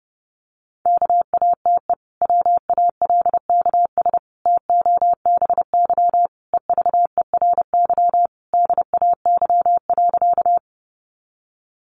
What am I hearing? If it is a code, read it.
Morse code, heard as a keyed tone, about 30 wpm, 700 Hz. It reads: KATE WALKS TOBY EVERY DAY.
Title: Easy Phrases 30wpm